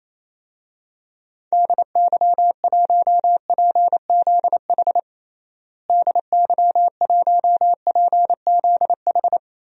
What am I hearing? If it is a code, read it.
Morse code, heard as a keyed tone, about 28 wpm, 700 Hz. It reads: DY1PZH DY1PZH
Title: DXCC-100 Call Signs - 6 Characters - Repeated 3x - 28wpm